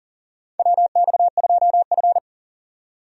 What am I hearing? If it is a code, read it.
Morse code, heard as a keyed tone, about 40 wpm, 700 Hz. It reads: WX2F
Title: USA Call Signs - 4 Characters - 40wpm